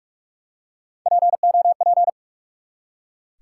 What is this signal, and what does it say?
Morse code, heard as a keyed tone, about 44 wpm, 700 Hz. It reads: POP
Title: Three Letter Words - 45wpm